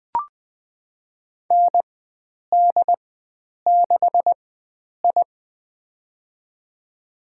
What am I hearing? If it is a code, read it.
Morse code, heard as a keyed tone, about 21 wpm, spaced as for 9 wpm, 700 Hz. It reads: ND6I
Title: USA Call Signs - 4 Characters - 4x Character Spacing - 20wpm